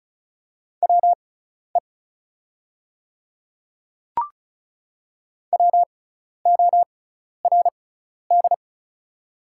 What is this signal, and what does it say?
Morse code, heard as a keyed tone, about 34 wpm, spaced as for 12 wpm, 700 Hz. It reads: WE WORD
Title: 6x Character Spacing - Top 100 Words - 35wpm